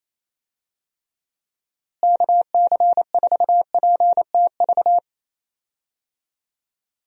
Morse code, heard as a keyed tone, about 28 wpm, 700 Hz. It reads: KC4PTV
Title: USA Call Signs - 6 Characters - Repeated 2x - 28wpm